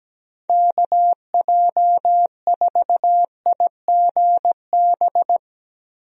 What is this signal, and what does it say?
Morse code, heard as a keyed tone, about 17 wpm, 700 Hz. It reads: KJ4IGB
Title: USA Call Signs - 6 Characters - Repeated 3x - 17wpm